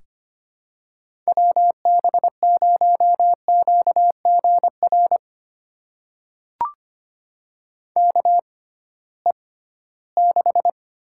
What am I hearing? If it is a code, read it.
Morse code, heard as a keyed tone, about 25 wpm, spaced as for 16 wpm, 700 Hz. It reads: WB0QGR KE6
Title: USA Call Signs - 6 Characters - 6x Character Spacing - 25wpm